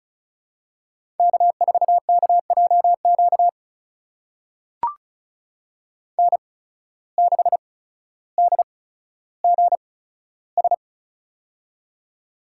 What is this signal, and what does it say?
Morse code, heard as a keyed tone, about 36 wpm, spaced as for 15 wpm, 700 Hz. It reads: K4KJQ N6DGS
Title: USA Call Signs - 5 Characters - 8x Character Spacing - 35wpm